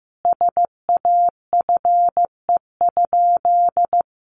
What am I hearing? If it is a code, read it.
Morse code, heard as a keyed tone, about 15 wpm, 700 Hz. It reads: SAFE?
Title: Sentences from Top 900 Words 15wpm